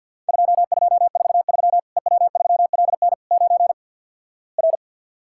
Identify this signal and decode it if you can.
Morse code, heard as a keyed tone, about 48 wpm, 700 Hz. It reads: 2243 EW3LN 9 R
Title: Worked All Europe DX Contest - QTC Exchange - 50wpm